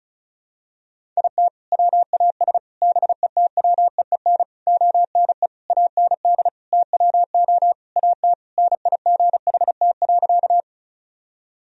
Morse code, heard as a keyed tone, about 34 wpm, 700 Hz. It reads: IT WAS BETWEEN ONE AND TWO AT NIGHT.
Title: Sentences from Top 1000 Words - Review Part 1 - 35wpm